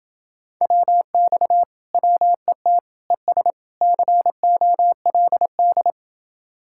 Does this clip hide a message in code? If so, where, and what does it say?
Yes: Morse code from 0.6 to 5.9 s, heard as a keyed tone, about 28 wpm, 700 Hz. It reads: WX WET ES COLD